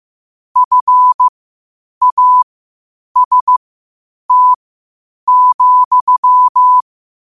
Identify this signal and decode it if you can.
Morse code, heard as a keyed tone, about 15 wpm, spaced as for 8 wpm, 1000 Hz. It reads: FAST,
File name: qotdfiles / 8wpm